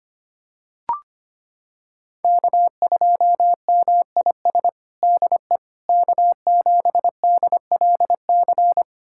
Morse code, heard as a keyed tone, about 25 wpm, 700 Hz. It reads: K2MIS DE K7DLC
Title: USA Callsign DE Callsign - 5 Characters - 25wpm